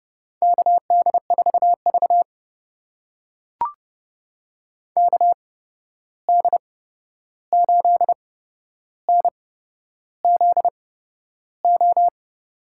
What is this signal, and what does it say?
Morse code, heard as a keyed tone, about 30 wpm, spaced as for 11 wpm, 700 Hz. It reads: KD4V KD8NZO
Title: USA Call Signs - 8x Character Spacing - 30wpm